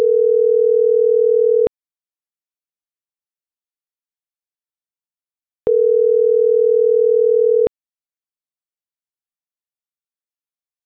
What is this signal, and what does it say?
Ringback tone (North America), 440 + 480 Hz.